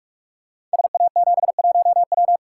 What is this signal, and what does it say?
Morse code, heard as a keyed tone, about 44 wpm, 700 Hz. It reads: SA71W
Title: Code Groups - 5 Letter:Numbers - 45wpm